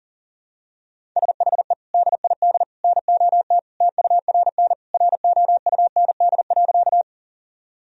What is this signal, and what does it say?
Morse code, heard as a keyed tone, about 40 wpm, 700 Hz. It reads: SHE DID NOT TURN ROUND.
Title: Sentences from Top 1000 Words - Review Part 1 - 40wpm